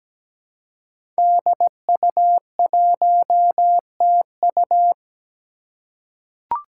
Morse code, heard as a keyed tone, about 17 wpm, 700 Hz. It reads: DU1TU
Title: Five Challenging Letter-Number Combinations 17wpm